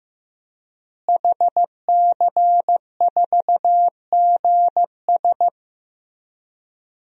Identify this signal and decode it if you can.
Morse code, heard as a keyed tone, about 15 wpm, 700 Hz. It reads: HC4GS
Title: DXCC-100 Call Signs - 5 Characters - Repeated 3x - 15wpm